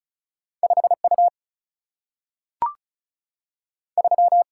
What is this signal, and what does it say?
Morse code, heard as a keyed tone, about 36 wpm, 700 Hz. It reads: 5U 3
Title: Single-Letter Single-Number 35wpm